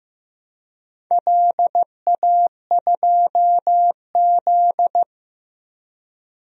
Morse code, heard as a keyed tone, about 15 wpm, 700 Hz. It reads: LA2Z
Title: DXCC-100 Call Signs - 4 Characters - Repeated 3x - 15wpm